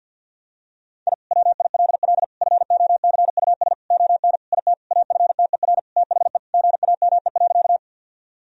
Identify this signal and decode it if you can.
Morse code, heard as a keyed tone, about 51 wpm, 700 Hz. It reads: I WILL FOCUS ON IT AFTER THE GAME.
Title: Sentences from Top 1000 Words - Review Part 4 - 50wpm